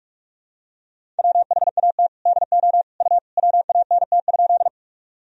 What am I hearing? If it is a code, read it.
Morse code, heard as a keyed tone, about 41 wpm, 700 Hz. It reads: WHAT DO U WANT?
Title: Sentences from Top 200 Words and Abbreviations 45wpm